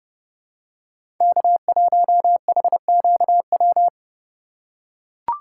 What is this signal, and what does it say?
Morse code, heard as a keyed tone, about 31 wpm, 700 Hz. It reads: K1HQW
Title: USA Call Signs - 5 Characters - 8x Character Spacing - 30wpm